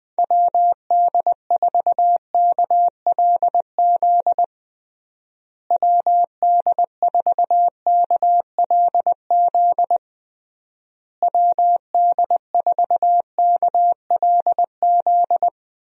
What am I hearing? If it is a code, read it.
Morse code, heard as a keyed tone, about 21 wpm, 700 Hz. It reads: WD4KLZ WD4KLZ WD4KLZ
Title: USA Call Signs - 6 Characters - Repeated 3x - 20wpm